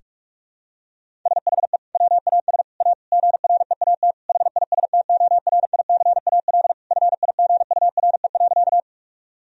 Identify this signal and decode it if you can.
Morse code, heard as a keyed tone, about 41 wpm, 700 Hz. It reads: SHE WAS A GREAT HISTORICAL FIGURE.
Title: Sentences from Top 1000 Words - Review Part 4 - 45wpm